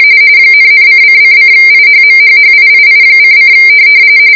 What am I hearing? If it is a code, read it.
RTTY, about 45.45 baud. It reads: ,,::::IHMHHIII::